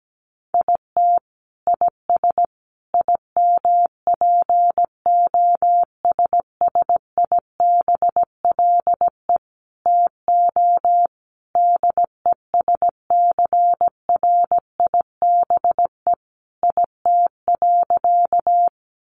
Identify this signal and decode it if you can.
Morse code, heard as a keyed tone, about 17 wpm, 700 Hz. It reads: IT IS IMPOSSIBLE TO DESCRIBE IT.
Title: Sentences from Top 1100 Words 17wpm